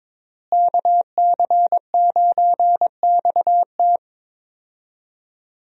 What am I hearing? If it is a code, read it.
Morse code, heard as a keyed tone, about 21 wpm, 700 Hz. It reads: KC9XT
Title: USA Call Signs - 5 Characters - Repeated 2x - 22wpm